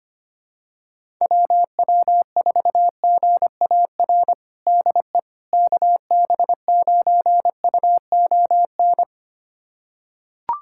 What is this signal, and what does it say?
Morse code, heard as a keyed tone, about 25 wpm, 700 Hz. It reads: WW4GAR DE KB9UON
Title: USA Callsign DE Callsign - 6 Characters - 25wpm